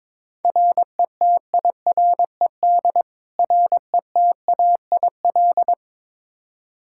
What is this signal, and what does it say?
Morse code, heard as a keyed tone, about 23 wpm, 700 Hz. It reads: RETIRED RETAIL
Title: K1USN SST QSO Practice 22wpm